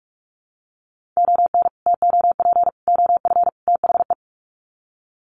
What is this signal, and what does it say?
Morse code, heard as a keyed tone, about 44 wpm, 700 Hz. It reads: ON TOP OF THE
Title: Sets of 4 Words 45wpm